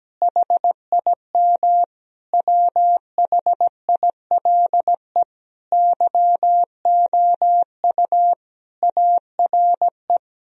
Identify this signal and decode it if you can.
Morse code, heard as a keyed tone, about 17 wpm, 700 Hz. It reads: HIM WHILE YOU ARE
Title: Sentences from Top 1000 Words - Review Part 1 - 17wpm